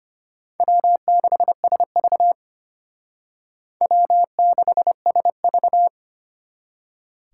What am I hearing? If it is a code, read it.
Morse code, heard as a keyed tone, about 32 wpm, 700 Hz. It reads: W6SV W6SV